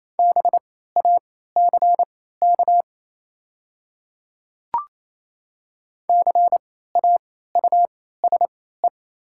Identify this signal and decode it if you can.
Morse code, heard as a keyed tone, about 28 wpm, spaced as for 16 wpm, 700 Hz. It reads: BACK CAUSE
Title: 3x Character Spacing - Top 200 Words - 28wpm